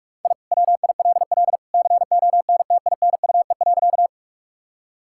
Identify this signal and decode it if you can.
Morse code, heard as a keyed tone, about 41 wpm, 700 Hz. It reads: I WILL CONTINUE.
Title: Sentences from Top 1000 Words - Review Part 6 - 45wpm